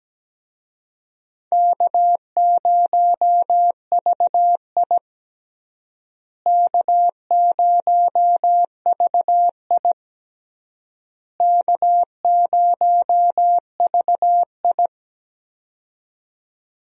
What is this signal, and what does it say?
Morse code, heard as a keyed tone, about 17 wpm, 700 Hz. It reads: K0VI K0VI K0VI
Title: USA Call Signs - 4 Characters - Repeated 3x - 17wpm